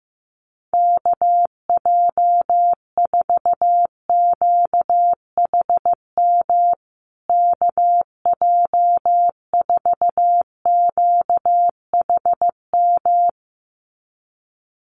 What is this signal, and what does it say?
Morse code, heard as a keyed tone, about 15 wpm, 700 Hz. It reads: KJ4QHM KJ4QHM
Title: USA Call Signs Repeated 2x 15wpm